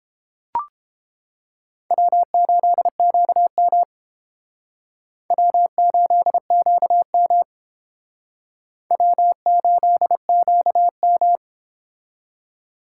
Morse code, heard as a keyed tone, about 31 wpm, 700 Hz. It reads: W8QM W8QM W8QM